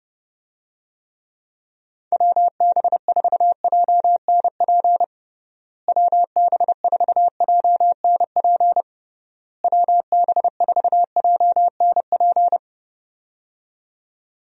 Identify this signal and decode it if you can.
Morse code, heard as a keyed tone, about 30 wpm, 700 Hz. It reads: WB4JNP WB4JNP WB4JNP
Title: USA Call Signs - 6 Characters - Repeated 3x - 30wpm